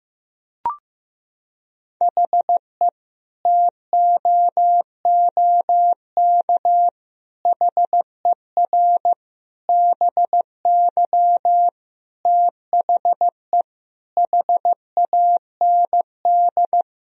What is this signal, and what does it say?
Morse code, heard as a keyed tone, about 15 wpm, 700 Hz. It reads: HE TOOK HER BY THE HAND
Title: Sentences from Top 1000 Words - Review Part 2 - 15wpm